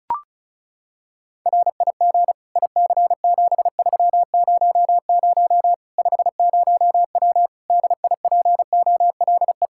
Morse code, heard as a keyed tone, about 36 wpm, 700 Hz. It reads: RIG IC7300 50W DIPOLE
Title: Practice QSO - Exchange 2 - Info - 35wpm